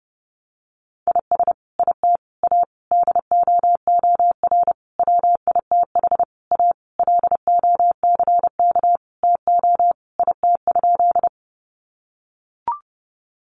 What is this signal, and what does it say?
Morse code, heard as a keyed tone, about 30 wpm, 700 Hz. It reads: IS IT A DOOR WITH A LOCK TO IT?
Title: Sentences from Top 1300 Words 30wpm